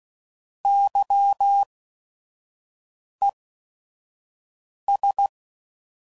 Morse code, heard as a keyed tone, about 16 wpm, spaced as for 5 wpm, 800 Hz. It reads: YES
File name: Prefixes / By First Letter / Ys